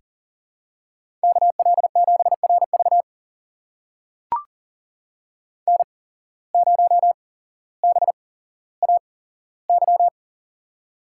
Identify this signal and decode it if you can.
Morse code, heard as a keyed tone, about 40 wpm, spaced as for 17 wpm, 700 Hz. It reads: KL7RV N0BAY
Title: USA Call Signs - 5 Characters - 8x Character Spacing - 40wpm